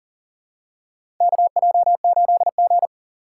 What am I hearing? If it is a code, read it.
Morse code, heard as a keyed tone, about 40 wpm, 700 Hz. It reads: KJ8G
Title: USA Call Signs - 4 Characters - 3x Character Spacing - 40wpm